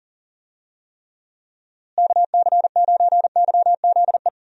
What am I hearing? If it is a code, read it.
Morse code, heard as a keyed tone, about 39 wpm, 700 Hz. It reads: KC9YZE
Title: USA Call Signs - 6 Characters - Repeated 3x - 40wpm